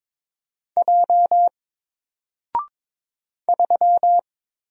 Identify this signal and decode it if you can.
Morse code, heard as a keyed tone, about 22 wpm, 700 Hz. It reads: J 3
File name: Single-Letter-Number-22wpm